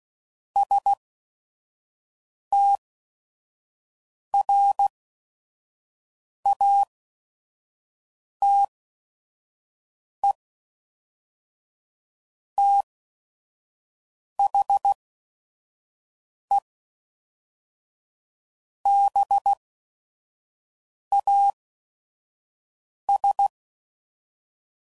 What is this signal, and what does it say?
Morse code, heard as a keyed tone, about 16 wpm, spaced as for 5 wpm, 800 Hz. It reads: STRATE THE BAS
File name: CW Code Groups @5WPM / Technical Subjects / AND Gate